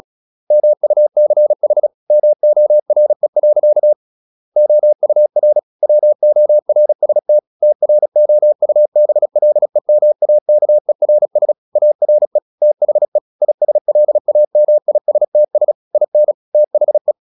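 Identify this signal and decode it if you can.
Morse code, heard as a keyed tone, about 36 wpm, 600 Hz. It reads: MUCH MORE. OUR WORST TROUBLEMAKERS ARE THE ISLAMISTS IN THE
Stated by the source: Morse practice at 36 wpm, text of Ken Follett's novel Never, from CW med Gnister.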